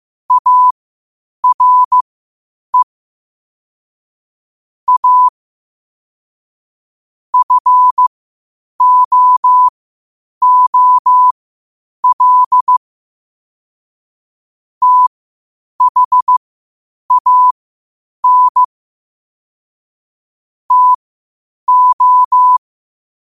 Morse code, heard as a keyed tone, about 15 wpm, spaced as for 8 wpm, 1000 Hz. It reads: ARE A FOOL THAN TO
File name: qotdfiles / 8wpm